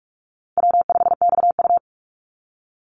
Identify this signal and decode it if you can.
Morse code, heard as a keyed tone, about 46 wpm, 700 Hz. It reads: W5XU
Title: Call Signs - 45wpm